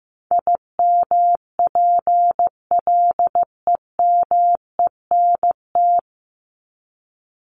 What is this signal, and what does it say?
Morse code, heard as a keyed tone, about 15 wpm, 700 Hz. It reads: IMPLEMENT
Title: Top 1501-1750 Words 15wpm